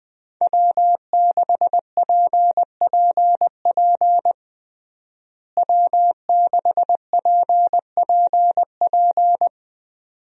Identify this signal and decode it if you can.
Morse code, heard as a keyed tone, about 20 wpm, 700 Hz. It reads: W6PPP W6PPP
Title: USA Call Signs - 5 Characters - Repeated 2x - 20wpm